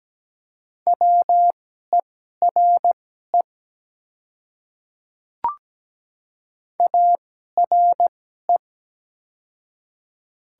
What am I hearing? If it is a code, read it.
Morse code, heard as a keyed tone, about 17 wpm, spaced as for 12 wpm, 700 Hz. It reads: WERE ARE
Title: 2x Character Spacing - Top 100 Words - 17wpm